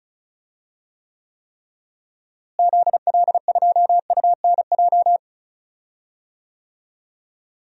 Morse code, heard as a keyed tone, about 36 wpm, 700 Hz. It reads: ZL2UNJ
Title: DXCC-100 Call Signs - 6 Characters - Repeated 3x - 35wpm